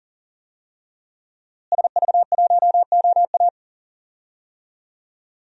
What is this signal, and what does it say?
Morse code, heard as a keyed tone, about 40 wpm, 700 Hz. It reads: SV1OA
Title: DXCC-100 Call Signs - 5 Characters - Repeated 3x - 40wpm